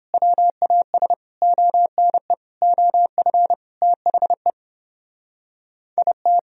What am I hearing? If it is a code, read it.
Morse code, heard as a keyed tone, about 29 wpm, 700 Hz. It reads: WAS ONE OF THE IT